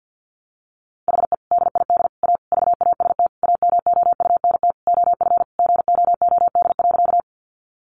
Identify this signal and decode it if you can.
Morse code, heard as a keyed tone, about 50 wpm, 700 Hz. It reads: HE DID A VAST AMOUNT OF GOOD.
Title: Sentences from Top 1300 Words 50wpm